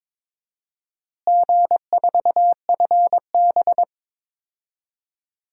Morse code, heard as a keyed tone, about 22 wpm, 700 Hz. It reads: G4FB
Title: DXCC-100 Call Signs - 4 Characters - Repeated 3x - 22wpm